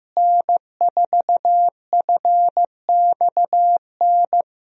Morse code, heard as a keyed tone, about 16 wpm, 700 Hz. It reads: N4FXN
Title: USA Call Signs - 5 Characters - Repeated 2x - 15wpm